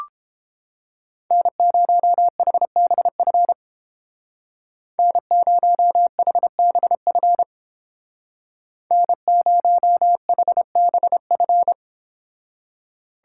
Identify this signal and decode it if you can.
Morse code, heard as a keyed tone, about 28 wpm, 700 Hz. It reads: N0HBF N0HBF N0HBF